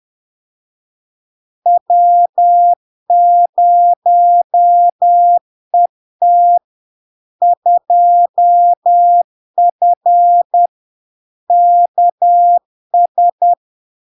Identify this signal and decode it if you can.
Morse code, heard as a keyed tone, about 10 wpm, 700 Hz. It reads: W0ET 2F KS
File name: LARK-FD-2022-Log-in-CW-10wpm-700-Hz